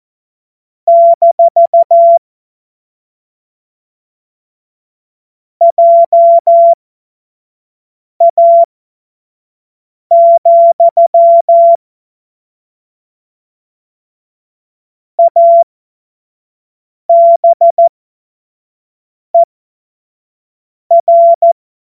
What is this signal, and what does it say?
Morse code, heard as a keyed tone, about 14 wpm, spaced as for 5 wpm, 680 Hz. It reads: - JA, ABER
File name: Morsekurs / Witze / Witz 27 Februar 2024 25 F70